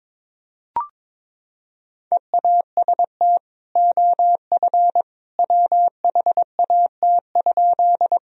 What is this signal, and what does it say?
Morse code, heard as a keyed tone, about 22 wpm, 700 Hz. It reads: EAST OF WHAT?
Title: Sentences from Top 1000 Words - Review Part 3 - 22wpm